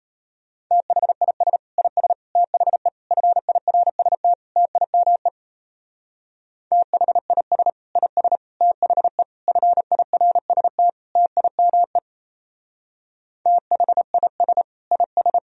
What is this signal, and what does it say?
Morse code, heard as a keyed tone, about 32 wpm, 700 Hz. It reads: THIS IS THE FIRST TIME THIS IS THE FIRST TIME THIS IS
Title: Speed-Racing - Sets of 5 Words 28wpm